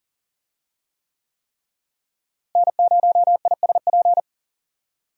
Morse code, heard as a keyed tone, about 37 wpm, 700 Hz. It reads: N0ISP
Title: USA Call Signs - 5 Characters - Repeated 3x - 40wpm